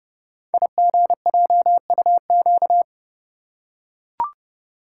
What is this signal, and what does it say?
Morse code, heard as a keyed tone, about 28 wpm, 690 Hz. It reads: IGJUQ
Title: Code Groups - 5 Letters - 30wpm